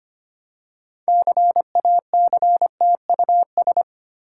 Morse code, heard as a keyed tone, about 25 wpm, 700 Hz. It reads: CACTUS
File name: practice / 25wpm / Difficult Words with Hints - 25wpm